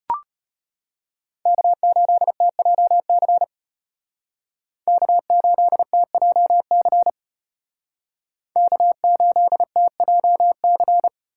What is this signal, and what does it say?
Morse code, heard as a keyed tone, about 33 wpm, 700 Hz. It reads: K8TJC K8TJC K8TJC